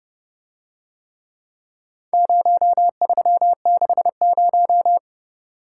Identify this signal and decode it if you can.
Morse code, heard as a keyed tone, about 29 wpm, 700 Hz. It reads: 0360